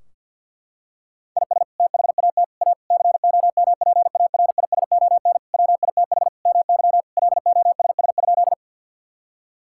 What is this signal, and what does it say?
Morse code, heard as a keyed tone, about 52 wpm, 700 Hz. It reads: IS THAT A COMPARISON WITH MY LOSS?
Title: Sentences from Top 1600 Words 50wpm